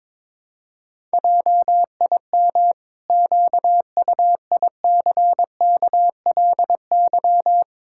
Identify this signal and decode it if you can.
Morse code, heard as a keyed tone, about 22 wpm, 700 Hz. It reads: JIM QUICKLY
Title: Pangram Challenge v2 - 22wpm